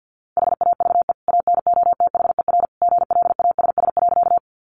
Morse code, heard as a keyed tone, about 50 wpm, 700 Hz. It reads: HAVE ANOTHER GLASS.
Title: Sentences from Top 1300 Words 50wpm